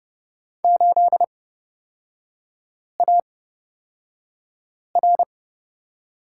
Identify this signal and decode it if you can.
Morse code, heard as a keyed tone, about 30 wpm, spaced as for 5 wpm, 700 Hz. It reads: 8AR
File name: Lesson 36e - BT - Three Characters 30wpm